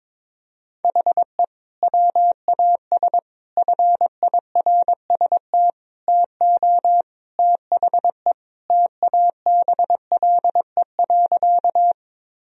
Morse code, heard as a keyed tone, about 22 wpm, 700 Hz. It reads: HE WAS FIRST TO THE TABLE.
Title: Sentences from Top 1000 Words - Review Part 2 - 22wpm